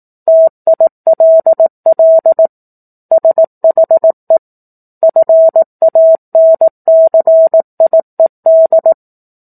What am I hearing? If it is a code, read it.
Morse code, heard as a keyed tone, about 18 wpm, 650 Hz. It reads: TILL SHE FANCIED
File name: 04-Chapter-04 18 WPM